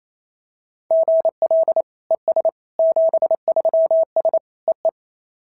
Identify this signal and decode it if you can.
Morse code, heard as a keyed tone, about 27 wpm, 690 Hz. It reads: GL ES 73S EE
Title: SOTA QSO Practice 28wpm